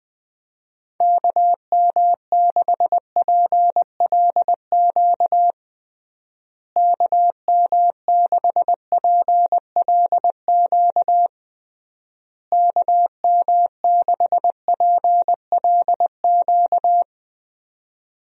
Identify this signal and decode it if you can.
Morse code, heard as a keyed tone, about 21 wpm, 700 Hz. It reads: KM6PLQ KM6PLQ KM6PLQ
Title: USA Call Signs - 6 Characters - Repeated 3x - 20wpm